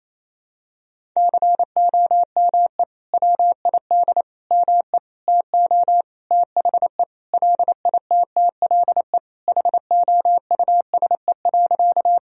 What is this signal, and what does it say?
Morse code, heard as a keyed tone, about 28 wpm, 700 Hz. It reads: COME WID ME TO THE LITTLE HOUSE.
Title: Sentences from Top 200 Words and Abbreviations 28wpm